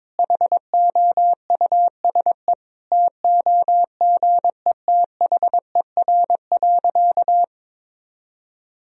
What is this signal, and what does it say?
Morse code, heard as a keyed tone, about 22 wpm, 700 Hz. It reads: HOUSE TOGETHER.
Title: Sentences from Top 1000 Words - Review Part 1 - 22wpm